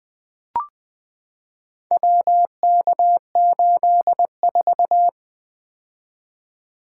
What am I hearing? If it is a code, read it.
Morse code, heard as a keyed tone, about 20 wpm, 700 Hz. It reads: WK84
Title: Four Letter-Number Combinations - 20wpm